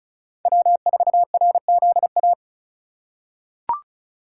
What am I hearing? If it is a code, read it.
Morse code, heard as a keyed tone, about 34 wpm, 700 Hz. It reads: W4RZA
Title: USA Call Signs - 5 Characters - 3x Character Spacing - 35wpm